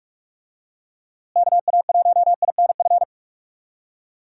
Morse code, heard as a keyed tone, about 43 wpm, 700 Hz. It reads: KA1INF
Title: USA Call Signs - 6 Characters - Repeated 3x - 45wpm